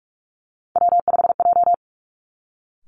Morse code, heard as a keyed tone, about 46 wpm, 700 Hz. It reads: W5J
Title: Three Challenging Letter-Number Combinations 45wpm